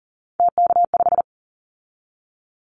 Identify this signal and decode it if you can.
Morse code, heard as a keyed tone, about 40 wpm, 700 Hz. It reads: TK5
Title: Callsign Prefixes for Top 100 DXCC - 40wpm